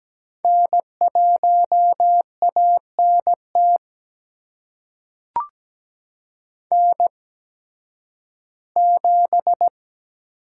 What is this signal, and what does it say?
Morse code, heard as a keyed tone, about 17 wpm, spaced as for 11 wpm, 700 Hz. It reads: N1ANT N7
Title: USA Call Signs - 5 Characters - 8x Character Spacing - 17wpm